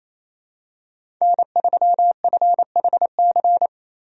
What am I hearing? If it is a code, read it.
Morse code, heard as a keyed tone, about 27 wpm, 700 Hz. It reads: N3FHC
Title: USA Call Signs - 5 Characters - 28wpm